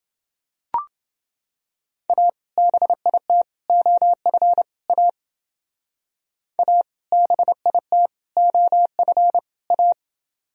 Morse code, heard as a keyed tone, about 29 wpm, 700 Hz. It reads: A BIT OF A A BIT OF A